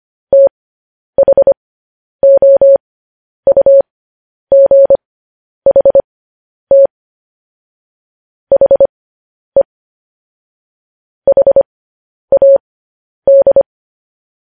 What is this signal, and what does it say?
Morse code, heard as a keyed tone, about 25 wpm, spaced as for 10 wpm, 550 Hz. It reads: THOUGHT HE HAD
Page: news headlines in morse code at 10 wpm